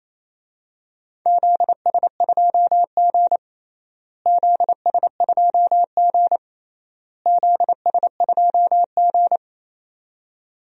Morse code, heard as a keyed tone, about 28 wpm, 700 Hz. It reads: ZS2G ZS2G ZS2G
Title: DXCC-100 Call Signs - 4 Characters - Repeated 3x - 28wpm